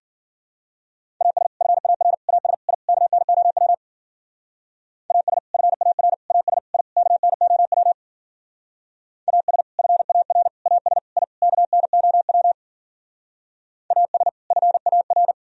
Morse code, heard as a keyed tone, about 49 wpm, 700 Hz. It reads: AS FAR AS I KNOW AS FAR AS I KNOW AS FAR AS I KNOW AS FAR
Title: Speed-Racing - Sets of 5 Words 40wpm